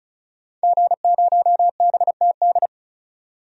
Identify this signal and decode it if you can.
Morse code, heard as a keyed tone, about 36 wpm, 700 Hz. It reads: G0BTD
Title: DXCC-100 Call Signs - 5 Characters - 35wpm